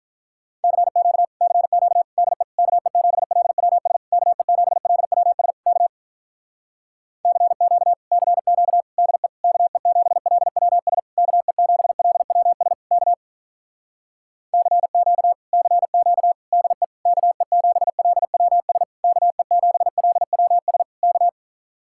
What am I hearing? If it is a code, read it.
Morse code, heard as a keyed tone, about 46 wpm, 700 Hz. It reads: CQ CQ DE KE7LWS KE7LWS K CQ CQ DE KE7LWS KE7LWS K CQ CQ DE KE7LWS KE7LWS K